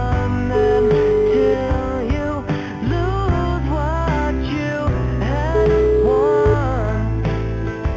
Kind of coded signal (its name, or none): telephone call progress tone